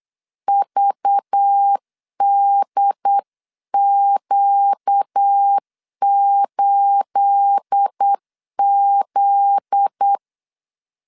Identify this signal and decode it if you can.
Morse code, heard as a keyed tone, about 8 wpm, 800 Hz. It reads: VDQ8Z
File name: LEZIONI / 10 SIGNS / 10A SLASH 8wpm 3452